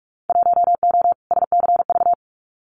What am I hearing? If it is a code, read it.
Morse code, heard as a keyed tone, about 46 wpm, 700 Hz. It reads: 1O SCV
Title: Winter Field Day Exchanges 45wpm